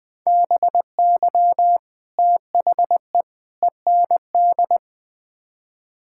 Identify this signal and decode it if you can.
Morse code, heard as a keyed tone, about 20 wpm, 700 Hz. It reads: BY THE END
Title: Build A Phrase - 5-Word Phrases - 20wpm